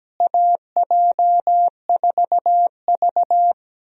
Morse code, heard as a keyed tone, about 17 wpm, 700 Hz. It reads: AJ4V
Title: USA Call Signs - 4 Characters - Repeated 3x - 17wpm